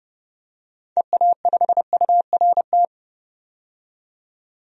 Morse code, heard as a keyed tone, about 31 wpm, 700 Hz. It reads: EA5URT
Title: DXCC-100 Call Signs - 6 Characters - 30wpm